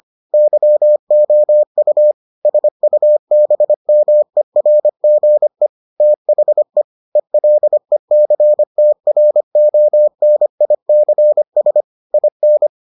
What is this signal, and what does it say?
Morse code, heard as a keyed tone, about 25 wpm, 600 Hz. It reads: YOU SUBMERGE THE ELECTRONICS IN